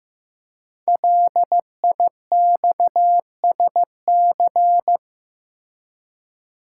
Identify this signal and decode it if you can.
Morse code, heard as a keyed tone, about 15 wpm, 700 Hz. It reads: LIXSC
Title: Code Groups - 5 Letters - 15wpm